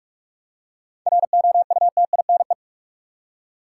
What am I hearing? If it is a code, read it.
Morse code, heard as a keyed tone, about 47 wpm, 700 Hz. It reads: ROUTINE
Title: Seven Letter Words - 45wpm